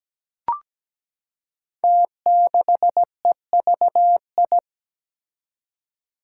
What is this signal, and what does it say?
Morse code, heard as a keyed tone, about 17 wpm, 700 Hz. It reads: T6EVI
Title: Five Challenging Letter-Number Combinations 17wpm